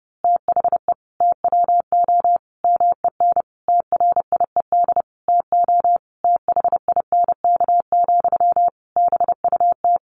Morse code, heard as a keyed tone, about 30 wpm, 700 Hz. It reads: THE TWO MEN TRIED TO THINK, BUT
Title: A Sound Of Thunder 30wpm